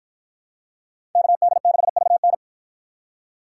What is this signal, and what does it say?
Morse code, heard as a keyed tone, about 53 wpm, 700 Hz. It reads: KD6VN